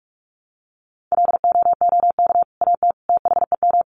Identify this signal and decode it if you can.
Morse code, heard as a keyed tone, about 44 wpm, 700 Hz. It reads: LOOK AT THEM
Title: Sentences from Top 1300 Words 45wpm